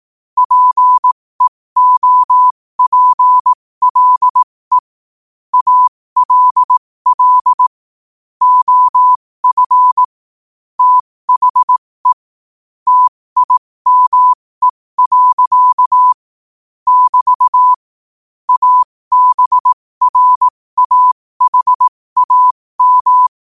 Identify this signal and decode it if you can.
Morse code, heard as a keyed tone, about 18 wpm, spaced as for 15 wpm, 1000 Hz. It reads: PEOPLE ALL OF THE TIME. <BT> ABRAHAM
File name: 15wpm